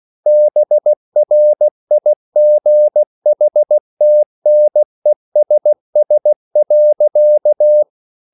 Morse code, heard as a keyed tone, about 16 wpm, 600 Hz. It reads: BRIGHTNESS.